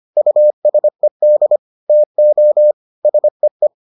Morse code, heard as a keyed tone, about 25 wpm, 600 Hz. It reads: USED TO SEE